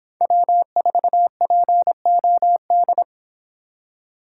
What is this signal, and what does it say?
Morse code, heard as a keyed tone, about 29 wpm, 700 Hz. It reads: W4POD